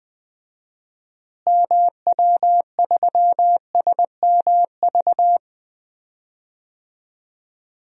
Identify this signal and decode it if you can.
Morse code, heard as a keyed tone, about 20 wpm, 700 Hz. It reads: MW3SMV
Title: DXCC-100 Call Signs - 6 Characters - Repeated 2x - 20wpm